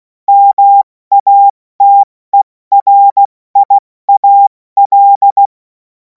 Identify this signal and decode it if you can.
Morse code, heard as a keyed tone, about 16 wpm, 800 Hz. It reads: MATERIAL
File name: Morsekurs / Witze / Witz 10. Februar 2026 70 F80